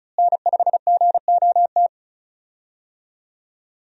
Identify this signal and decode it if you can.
Morse code, heard as a keyed tone, about 36 wpm, 700 Hz. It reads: N5GOT
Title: USA Call Signs - 5 Characters - Repeated 2x - 35wpm